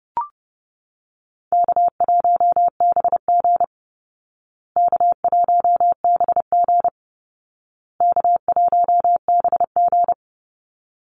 Morse code, heard as a keyed tone, about 30 wpm, 700 Hz. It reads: K1BG K1BG K1BG
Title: Easy Call Signs Repeated 3x - 30wpm